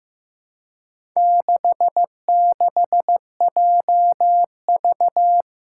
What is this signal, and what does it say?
Morse code, heard as a keyed tone, about 16 wpm, 700 Hz. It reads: 66JV
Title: Four Challenging Letter-Number Combinations 15wpm